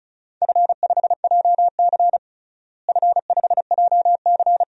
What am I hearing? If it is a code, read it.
Morse code, heard as a keyed tone, about 34 wpm, 700 Hz. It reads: F5JC F5JC
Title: DXCC-100 Call Signs - 4 Characters - Repeated 3x - 35wpm